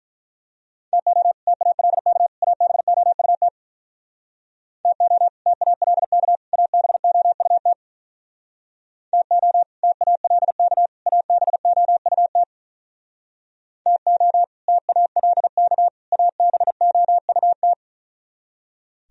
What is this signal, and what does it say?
Morse code, heard as a keyed tone, about 43 wpm, 700 Hz. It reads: TO TALK ABOUT TO TALK ABOUT TO TALK ABOUT TO TALK ABOUT
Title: Speed-Racing - Sets of 3 Words 35wpm